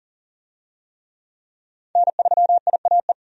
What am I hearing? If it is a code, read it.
Morse code, heard as a keyed tone, about 40 wpm, 700 Hz. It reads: N3IAE
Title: USA Call Signs - 5 Characters - 40wpm